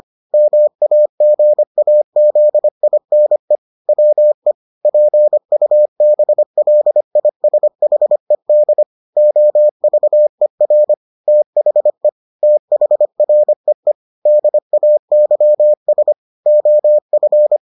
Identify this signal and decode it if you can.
Morse code, heard as a keyed tone, about 25 wpm, 600 Hz. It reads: MAGAZINE WE PUBLISHED OVER THE THREE DAYS OF